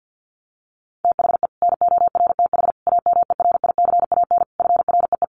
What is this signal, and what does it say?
Morse code, heard as a keyed tone, about 50 wpm, 700 Hz. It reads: THE NORTH AMERICAN FREE
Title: Sets of 5 Words Encore 50wpm